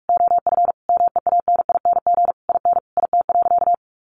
Morse code, heard as a keyed tone, about 44 wpm, 700 Hz. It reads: OF MEANING IN IT.
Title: Sentences from Top 1200 Words 45wpm